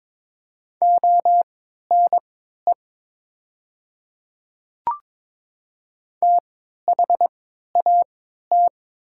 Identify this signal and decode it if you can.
Morse code, heard as a keyed tone, about 22 wpm, spaced as for 13 wpm, 700 Hz. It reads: ONE THAT
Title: 3x Character Spacing - Top 100 Words - 22wpm